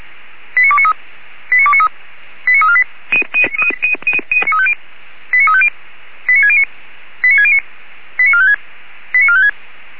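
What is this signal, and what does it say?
Five-tone selective call (ZVEI), heard as quick runs of five tones, tones of about 70 ms. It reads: 89293 89293 89348 89350 89609 89609 89456 89456
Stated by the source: ZVEI five-tone call received by SDR.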